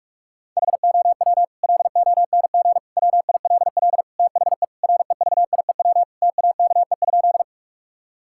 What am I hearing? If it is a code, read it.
Morse code, heard as a keyed tone, about 46 wpm, 700 Hz. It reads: HOW LONG WILL THE REVIEW TAKE?
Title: Sentences from Top 1000 Words - Review Part 6 - 45wpm